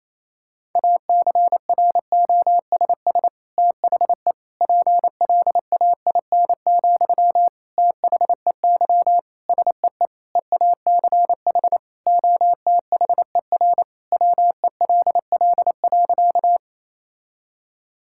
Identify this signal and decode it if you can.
Morse code, heard as a keyed tone, about 28 wpm, 700 Hz. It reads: ACROSS THE PLAIN, THEY SEE EACH OTHER WELL.
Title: Sentences from Top 1000 Words - Review Part 5 - 28wpm